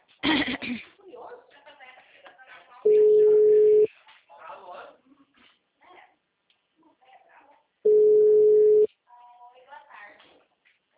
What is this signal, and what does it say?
Ringback tone (Europe, 425 Hz), with speech.